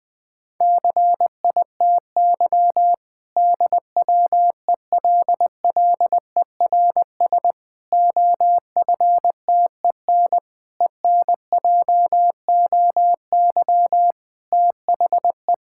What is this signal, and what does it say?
Morse code, heard as a keyed tone, about 20 wpm, 700 Hz. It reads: CITY DWELLERS OFTEN ENJOY THE
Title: Difficult Sentences with Hints - 20wpm